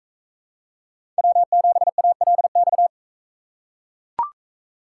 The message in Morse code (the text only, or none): W7ALX